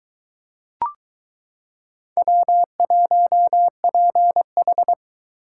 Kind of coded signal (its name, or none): Morse code